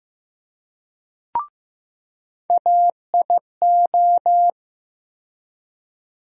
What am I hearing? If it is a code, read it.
Morse code, heard as a keyed tone, about 15 wpm, 700 Hz. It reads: AIO
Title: USA Call Sign Postfixes - 3 Letters 15wpm